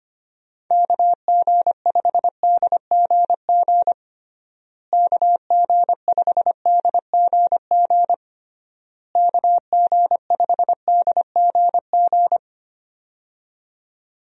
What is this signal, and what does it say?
Morse code, heard as a keyed tone, about 25 wpm, 700 Hz. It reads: KG5DGG KG5DGG KG5DGG
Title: USA Call Signs - 6 Characters - Repeated 3x - 25wpm